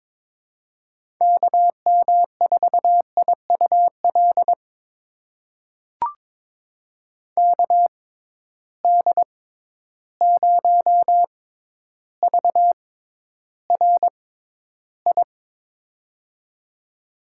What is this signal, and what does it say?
Morse code, heard as a keyed tone, about 22 wpm, spaced as for 11 wpm, 700 Hz. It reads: KM4IUL KD0VRI
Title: USA Call Signs - 6 Characters - 6x Character Spacing - 22wpm